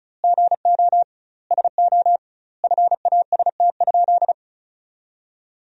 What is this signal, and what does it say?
Morse code, heard as a keyed tone, about 36 wpm, 700 Hz. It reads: GO SO FAST?
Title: 2x Word-Spacing - Sentences from Top 500 Words - 35wpm